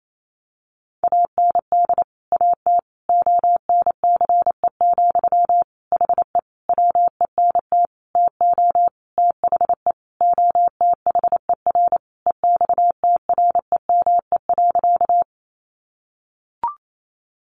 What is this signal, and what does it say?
Morse code, heard as a keyed tone, about 28 wpm, 700 Hz. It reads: AND AT ONCE, HE WENT TO THE OTHER EXTREME.
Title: Sentences from Top 1300 Words 28wpm